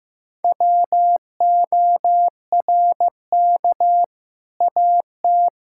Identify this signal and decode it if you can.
Morse code, heard as a keyed tone, about 16 wpm, 700 Hz. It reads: WORK AT
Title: Sentences from Top 1000 Words - Review Part 5 - 15wpm